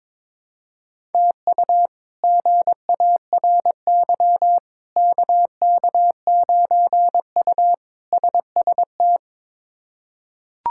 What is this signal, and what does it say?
Morse code, heard as a keyed tone, about 22 wpm, 700 Hz. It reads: TU GARY KK9U SST
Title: K1USN SST QSO Practice - 22wpm